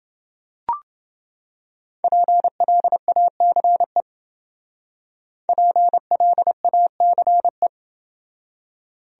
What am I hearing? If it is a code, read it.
Morse code, heard as a keyed tone, about 29 wpm, 700 Hz. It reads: PLACE PLACE